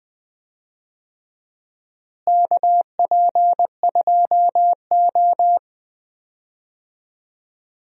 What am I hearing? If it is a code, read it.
Morse code, heard as a keyed tone, about 20 wpm, 700 Hz. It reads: KP2O
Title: DXCC-100 Call Signs - 4 Characters - Repeated 3x - 20wpm